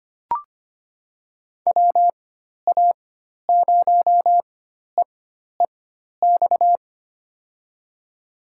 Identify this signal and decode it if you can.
Morse code, heard as a keyed tone, about 25 wpm, spaced as for 12 wpm, 700 Hz. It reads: WA0EEX
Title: USA Call Signs - 6 Characters - 4x Character Spacing - 25wpm